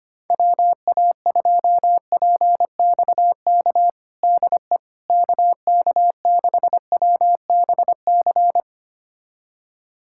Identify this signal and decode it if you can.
Morse code, heard as a keyed tone, about 25 wpm, 700 Hz. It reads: WA2PXK DE KK6WBC
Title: USA Callsign DE Callsign - 6 Characters - 25wpm